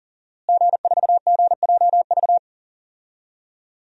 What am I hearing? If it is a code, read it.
Morse code, heard as a keyed tone, about 40 wpm, 700 Hz. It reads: G4GJV
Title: DXCC-100 Call Signs - 5 Characters - Repeated 3x - 40wpm